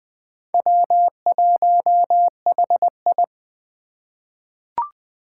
Morse code, heard as a keyed tone, about 20 wpm, 700 Hz. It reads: W1HI
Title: Four Challenging Letter-Number Combinations 20wpm